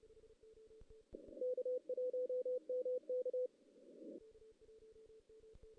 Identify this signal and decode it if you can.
Morse code, heard as a keyed tone, about 29 wpm, 500 Hz. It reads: K1MK
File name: 2016 JAN CW NAQP / 2016 Jan naqp k5ot - 0-2